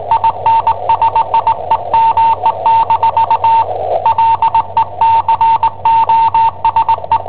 Morse code, heard as a keyed tone, about 21 wpm, 910 Hz. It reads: INSIEME- LE COSE
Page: Cassetta esercizi preparata da I7OHP per IW5ELP